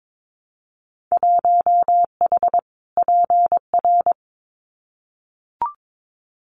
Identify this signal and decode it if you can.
Morse code, heard as a keyed tone, about 22 wpm, 700 Hz. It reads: 1H PR
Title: Winter Field Day Exchanges Encore 22wpm